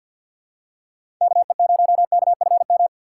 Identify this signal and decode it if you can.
Morse code, heard as a keyed tone, about 49 wpm, 700 Hz. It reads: KE0KFM
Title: USA Call Signs - 6 Characters - 4x Character Spacing - 50wpm